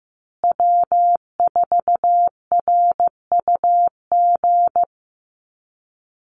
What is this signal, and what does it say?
Morse code, heard as a keyed tone, about 15 wpm, 700 Hz. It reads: W4RUG
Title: USA Call Signs Repeated 2x 15wpm